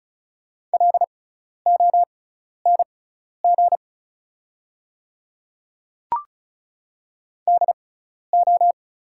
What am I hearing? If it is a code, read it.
Morse code, heard as a keyed tone, about 36 wpm, spaced as for 12 wpm, 700 Hz. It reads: LONG DO
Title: 6x Character Spacing - Top 100 Words - 35wpm